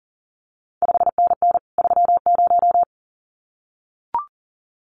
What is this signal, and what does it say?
Morse code, heard as a keyed tone, about 41 wpm, 700 Hz. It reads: 5NN 30
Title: CQ WW DX Contest Exchange Encore - 40wpm